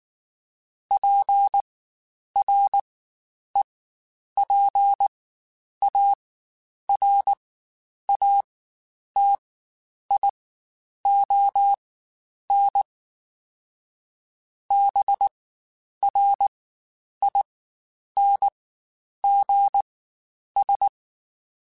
Morse code, heard as a keyed tone, about 19 wpm, spaced as for 9 wpm, 800 Hz. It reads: PREPARATION BRINGS
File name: Quotes Spaced / Quotes Spaced-mp3s / Persperation sp @18